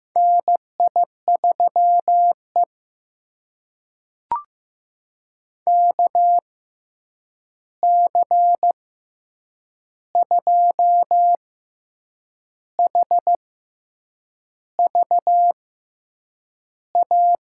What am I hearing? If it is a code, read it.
Morse code, heard as a keyed tone, about 15 wpm, spaced as for 7 wpm, 700 Hz. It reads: NI3E KC2HVA
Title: USA Call Signs - 6x Character Spacing - 15wpm